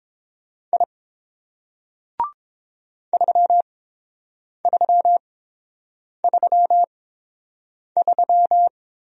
Morse code, heard as a keyed tone, about 29 wpm, spaced as for 7 wpm, 690 Hz. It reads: I 3333